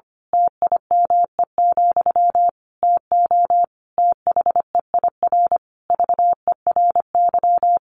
Morse code, heard as a keyed tone, about 25 wpm, 700 Hz. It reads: TIME, TO THEIR VERY
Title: A Sound Of Thunder 25wpm